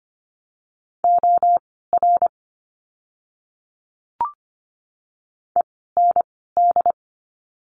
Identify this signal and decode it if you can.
Morse code, heard as a keyed tone, about 24 wpm, spaced as for 16 wpm, 700 Hz. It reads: OR END
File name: course / 25:10wpm / 094 Lesson 16f - Letter c - Words 25wpm